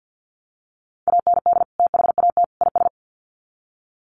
Morse code, heard as a keyed tone, about 50 wpm, 700 Hz. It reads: AND THAT IS
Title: Sets of 3 Words 50wpm